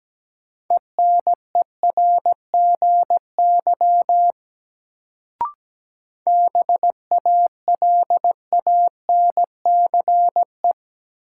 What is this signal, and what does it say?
Morse code, heard as a keyed tone, about 17 wpm, 700 Hz. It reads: ENERGY BALANCE
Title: Sentences from Top 900 Words - New Word Drill - 17wpm